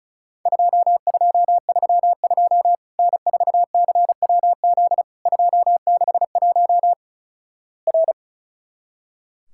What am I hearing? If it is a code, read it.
Morse code, heard as a keyed tone, about 36 wpm, 700 Hz. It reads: 2232 N4CWZ 261 R
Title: Worked All Europe DX Contest - QTC Exchange - 35wpm